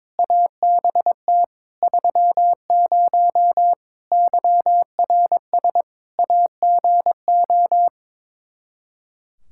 Morse code, heard as a keyed tone, about 22 wpm, 700 Hz. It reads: ABT 30 YRS AGO
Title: K1USN SST QSO Practice 22wpm